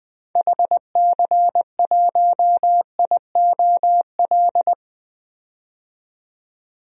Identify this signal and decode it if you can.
Morse code, heard as a keyed tone, about 20 wpm, 700 Hz. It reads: HC1IOL
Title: DXCC-100 Call Signs - 6 Characters - 20wpm